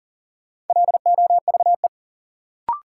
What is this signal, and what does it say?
Morse code, heard as a keyed tone, about 40 wpm, 700 Hz. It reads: LOVE
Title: Sets of 5 Words Encore - New Word Drill - 40wpm